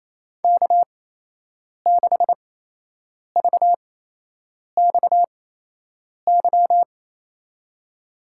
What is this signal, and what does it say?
Morse code, heard as a keyed tone, about 28 wpm, spaced as for 8 wpm, 700 Hz. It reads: K6VXY
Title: USA Call Signs - 8x Character Spacing - 28wpm